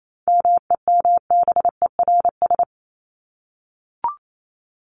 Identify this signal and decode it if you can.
Morse code, heard as a keyed tone, about 28 wpm, 700 Hz. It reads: MEMBERS
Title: ES:S Suffix - Top 100 Words with Base Words 28wpm